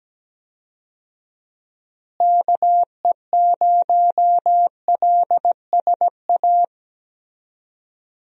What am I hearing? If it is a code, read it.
Morse code, heard as a keyed tone, about 17 wpm, 700 Hz. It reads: KE0LSA
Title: USA Call Signs - 6 Characters - Repeated 2x - 17wpm